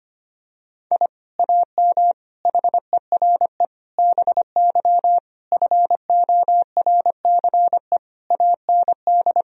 Morse code, heard as a keyed tone, about 25 wpm, 700 Hz. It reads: I AM HERE BY FORCE AND
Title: Sentences from Top 1000 Words - Review Part 3 - 25wpm